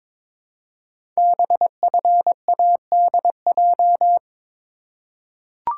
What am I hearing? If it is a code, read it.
Morse code, heard as a keyed tone, about 22 wpm, 700 Hz. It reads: BFADJ
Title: Code Groups - 5 Letters - 22wpm